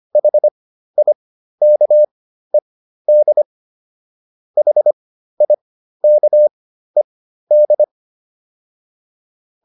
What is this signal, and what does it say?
Morse code, heard as a keyed tone, about 25 wpm, spaced as for 13 wpm, 600 Hz. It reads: HIKED HIKED